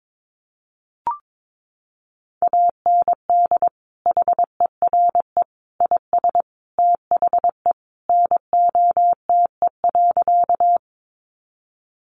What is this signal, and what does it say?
Morse code, heard as a keyed tone, about 22 wpm, 700 Hz. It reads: AND HERE IS THE NOTE.
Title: Sentences from Top 500 Words 22wpm